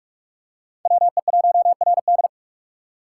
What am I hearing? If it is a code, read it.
Morse code, heard as a keyed tone, about 44 wpm, 700 Hz. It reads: WE1RD